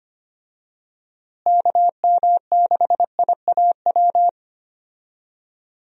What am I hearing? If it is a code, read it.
Morse code, heard as a keyed tone, about 25 wpm, 700 Hz. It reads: KM6IAW
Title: USA Call Signs - 6 Characters - Repeated 2x - 25wpm